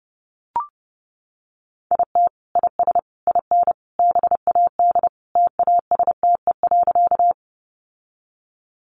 Morse code, heard as a keyed tone, about 30 wpm, 700 Hz. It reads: IT IS IN BAD TASTE.
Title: Sentences from Top 1100 Words 30wpm